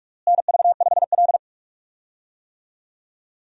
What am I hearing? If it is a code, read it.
Morse code, heard as a keyed tone, about 43 wpm, 700 Hz. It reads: NV5L
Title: USA Call Signs - 4 Characters - Repeated 3x - 45wpm